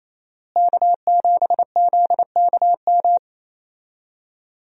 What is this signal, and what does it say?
Morse code, heard as a keyed tone, about 28 wpm, 700 Hz. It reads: K7ZKM
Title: USA Call Signs - 5 Characters - Repeated 2x - 28wpm